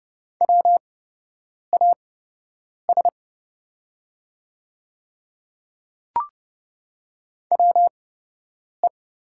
Morse code, heard as a keyed tone, about 30 wpm, spaced as for 8 wpm, 700 Hz. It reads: WAS WE
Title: 8x Character Spacing - Top 100 Words - 30wpm